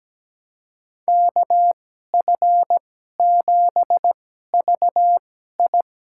Morse code, heard as a keyed tone, about 17 wpm, spaced as for 12 wpm, 700 Hz. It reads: KF7VI
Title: USA Call Signs - 2x Character Spacing - 17wpm